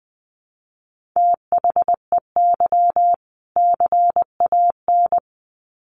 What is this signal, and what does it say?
Morse code, heard as a keyed tone, about 20 wpm, 700 Hz. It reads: THEY CAN
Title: Sets of 2 Words 20wpm